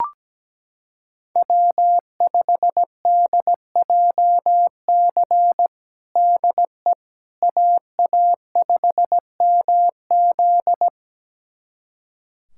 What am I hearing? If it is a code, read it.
Morse code, heard as a keyed tone, about 17 wpm, 700 Hz. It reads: W5DJC DE AA5MZ
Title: USA Callsign DE Callsign - 5 Characters - 17wpm